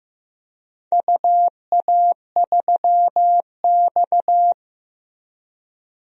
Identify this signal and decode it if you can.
Morse code, heard as a keyed tone, about 15 wpm, 700 Hz. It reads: UA3X
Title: DXCC-100 Call Signs - 4 Characters - Repeated 2x - 15wpm